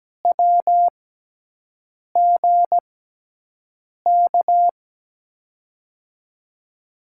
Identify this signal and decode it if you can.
Morse code, heard as a keyed tone, about 17 wpm, spaced as for 6 wpm, 700 Hz. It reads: WGK
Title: USA Call Signs - 6 Characters - 6x Character Spacing - 17wpm